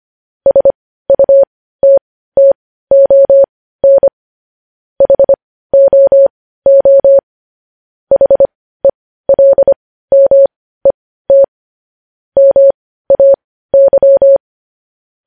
Morse code, heard as a keyed tone, about 25 wpm, spaced as for 15 wpm, 550 Hz. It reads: SUTTON HOO HELMET MAY
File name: podcast / News headlines-1x-25x15